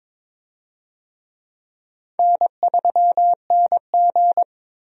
Morse code, heard as a keyed tone, about 22 wpm, 700 Hz. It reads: N3NG
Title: USA Call Signs - 4 Characters - 22wpm